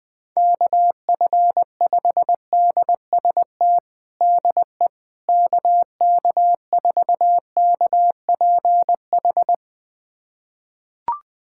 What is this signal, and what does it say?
Morse code, heard as a keyed tone, about 21 wpm, 700 Hz. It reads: KF5DST DE KK4KPH
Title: USA Callsign DE Callsign 20wpm